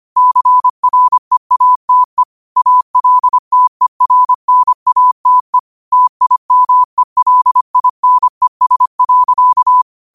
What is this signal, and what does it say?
Morse code, heard as a keyed tone, about 25 wpm, 1000 Hz. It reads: CREATE ALTERNATE TIMELINES.